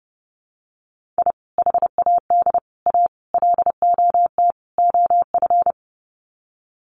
Morse code, heard as a keyed tone, about 30 wpm, 700 Hz. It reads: I HAD A LOT OF
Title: Sets of 5 Words Encore 30wpm